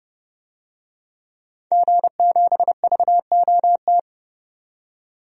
Morse code, heard as a keyed tone, about 30 wpm, 700 Hz. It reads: G7VOT
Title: DXCC-100 Call Signs - 5 Characters - 30wpm